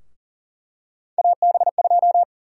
Morse code, heard as a keyed tone, about 40 wpm, 700 Hz. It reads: AB2
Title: USA Call Sign Prefixes - 40wpm